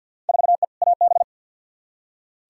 Morse code, heard as a keyed tone, about 46 wpm, 700 Hz. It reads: 4E AB
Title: Field Day Exchanges Encore - 50wpm